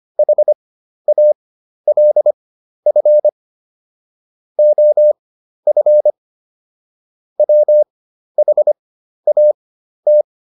Morse code, heard as a keyed tone, about 25 wpm, spaced as for 12 wpm, 600 Hz. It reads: HALF OF WHAT